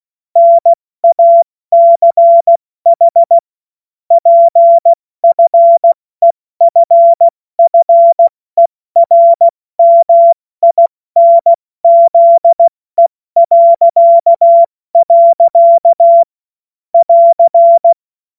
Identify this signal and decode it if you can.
Morse code, heard as a keyed tone, about 16 wpm, 680 Hz. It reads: NACH PFEFFERMINZE.. <AR>
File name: Morsekurs / Witze / Witz 10 Juni Mai 2025 70 F80